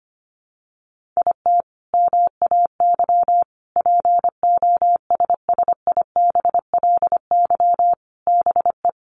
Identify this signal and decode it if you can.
Morse code, heard as a keyed tone, about 25 wpm, 700 Hz. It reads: IT MAY POSSIBLY BE
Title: Sentences from Top 1300 Words 25wpm